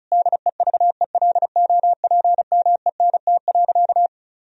Morse code, heard as a keyed tone, about 34 wpm, 700 Hz. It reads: DEVELOPMENT.
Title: Sentences from Top 1000 Words - Review Part 4 - 35wpm